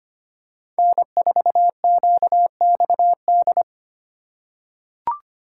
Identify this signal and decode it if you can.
Morse code, heard as a keyed tone, about 25 wpm, 700 Hz. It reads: N4QXD
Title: USA Call Signs - 5 Characters - 3x Character Spacing - 25wpm